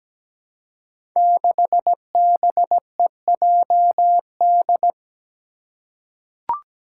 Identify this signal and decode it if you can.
Morse code, heard as a keyed tone, about 17 wpm, 700 Hz. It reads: 6BEJD
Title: Five Challenging Letter-Number Combinations 17wpm